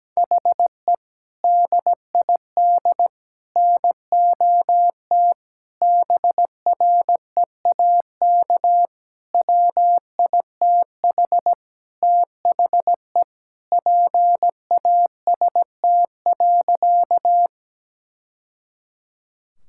Morse code, heard as a keyed tone, about 17 wpm, 700 Hz. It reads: HE DID NOT BREAK WITH THE PAST.
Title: Sentences from Top 1000 Words - Review Part 6 - 17wpm